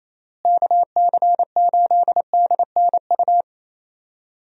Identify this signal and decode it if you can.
Morse code, heard as a keyed tone, about 27 wpm, 700 Hz. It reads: KC8DNU
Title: USA Call Signs - 6 Characters - Repeated 3x - 28wpm